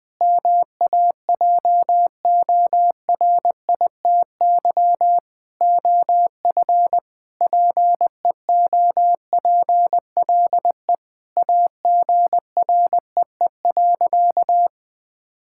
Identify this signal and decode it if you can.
Morse code, heard as a keyed tone, about 20 wpm, 700 Hz. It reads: MAJORITY OF PEOPLE AGREE.
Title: Sentences with 3-Word Phrases - Spoken Beginning - 20wpm